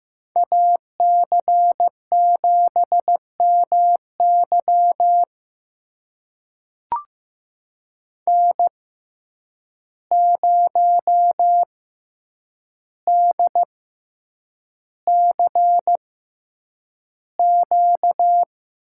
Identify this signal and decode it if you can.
Morse code, heard as a keyed tone, about 15 wpm, spaced as for 8 wpm, 700 Hz. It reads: AC7MY N0DCQ
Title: USA Call Signs - 5 Characters - 6x Character Spacing - 15wpm